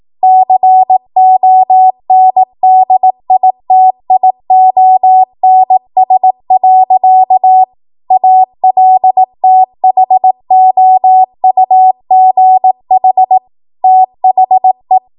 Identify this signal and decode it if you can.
Morse code, heard as a keyed tone, about 18 wpm, 750 Hz. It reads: CONDITIONS. ALTHOUGH THE
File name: Morse / 18WPM